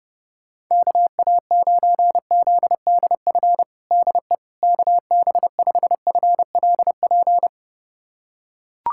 Morse code, heard as a keyed tone, about 29 wpm, 700 Hz. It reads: KA9ZDF DE KB5FLP
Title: USA Callsign DE Callsign - 6 Characters - 30wpm